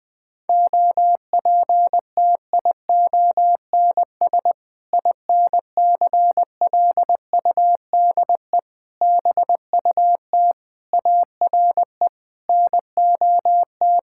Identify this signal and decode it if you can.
Morse code, heard as a keyed tone, about 20 wpm, 700 Hz. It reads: OPTIONS INCLUDE BUT ARE NOT
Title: Sentences with 6-Word Phrases - 20wpm